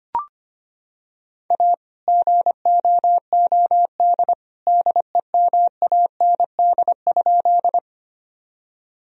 Morse code, heard as a keyed tone, about 25 wpm, 700 Hz. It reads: A GOOD DEMAND?
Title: Sentences from Top 1000 Words - Review Part 4 - 25wpm